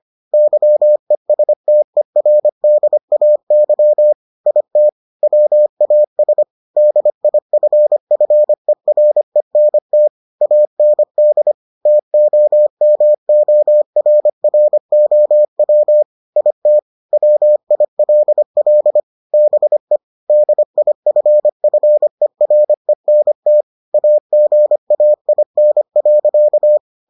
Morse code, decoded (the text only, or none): YESTERDAY IT WAS DIFFERENT AND TOMORROW IT WILL BE DIFFERENT AGAIN.